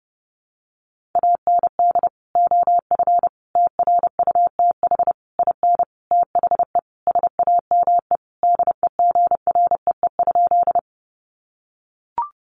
Morse code, heard as a keyed tone, about 30 wpm, 700 Hz. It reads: AND OF TRUTH IN THE SAME DEGREE?
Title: Sentences from Top 1500 Words 30wpm